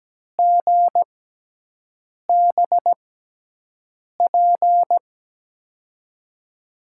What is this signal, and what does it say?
Morse code, heard as a keyed tone, about 17 wpm, spaced as for 6 wpm, 700 Hz. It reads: GBP
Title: USA Call Signs - 6 Characters - 6x Character Spacing - 17wpm